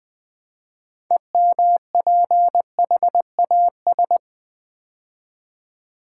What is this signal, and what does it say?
Morse code, heard as a keyed tone, about 20 wpm, 700 Hz. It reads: EMPHAS
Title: Build A Word - 3 Letters At A Time - 8-Letter Words - 20wpm